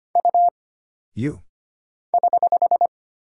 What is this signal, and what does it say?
Morse code, heard as a keyed tone, about 25 wpm, 700 Hz. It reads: U <HH>
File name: Single Character (Letter:Number:Punctuation:Prosign) - Mind-Melt - 25wpm